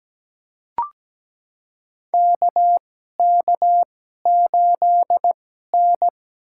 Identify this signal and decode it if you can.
Morse code, heard as a keyed tone, about 17 wpm, spaced as for 12 wpm, 700 Hz. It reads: KK8N
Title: USA Call Signs - 4 Characters - 2x Character Spacing - 17wpm